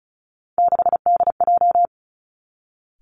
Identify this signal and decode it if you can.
Morse code, heard as a keyed tone, about 36 wpm, 700 Hz. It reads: 6DJ
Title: Three Challenging Letter-Number Combinations 35wpm